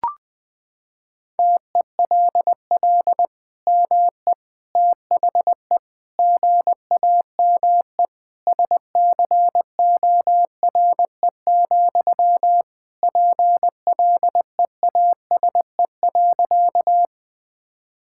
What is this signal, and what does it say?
Morse code, heard as a keyed tone, about 20 wpm, 700 Hz. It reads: TELL ME THE GAME SCORE, PLEASE.
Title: Sentences from Top 1600 Words 20wpm